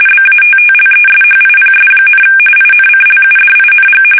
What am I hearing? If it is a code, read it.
RTTY, about 45.45 baud. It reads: BATTLESHIPS, INCLUDING
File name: fsk850 / battleshipnj